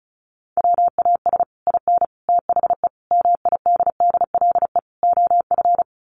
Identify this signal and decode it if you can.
Morse code, heard as a keyed tone, about 34 wpm, 700 Hz. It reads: WAS IN THE MIDDLE OF
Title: Sets of 5 Words Encore 35wpm